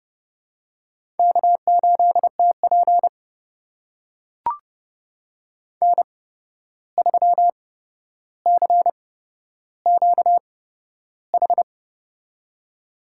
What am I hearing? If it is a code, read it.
Morse code, heard as a keyed tone, about 30 wpm, spaced as for 12 wpm, 700 Hz. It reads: K8TP N3CQH
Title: USA Call Signs - 8x Character Spacing - 30wpm